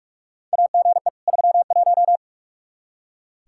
Morse code, heard as a keyed tone, about 46 wpm, 700 Hz. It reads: AGE 31
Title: Practice QSO - Exchange 3 - More Info - 45wpm